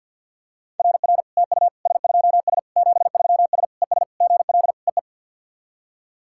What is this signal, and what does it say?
Morse code, heard as a keyed tone, about 54 wpm, 700 Hz. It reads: RR TU S2S 73S ES GL EE
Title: SOTA QSO Practice 50wpm